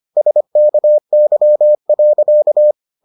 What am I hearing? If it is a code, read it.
Morse code, heard as a keyed tone, about 25 wpm, 600 Hz. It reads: SKY.